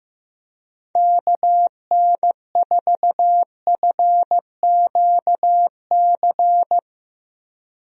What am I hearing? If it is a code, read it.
Morse code, heard as a keyed tone, about 15 wpm, 700 Hz. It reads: KN4FQC
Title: USA Call Signs - 6 Characters - Repeated 2x - 15wpm